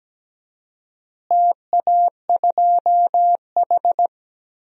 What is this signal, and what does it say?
Morse code, heard as a keyed tone, about 17 wpm, 700 Hz. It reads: TA2H
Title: DXCC-100 Call Signs - 4 Characters - Repeated 2x - 17wpm